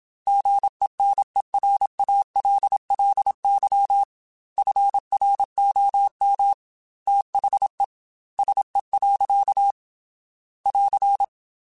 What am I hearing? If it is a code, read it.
Morse code, heard as a keyed tone, about 27 wpm, 800 Hz. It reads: GENERALLY FROM THE SE. <AR>
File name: Weather Sets / Weather Sets-mp3s / Weather 4 17 @25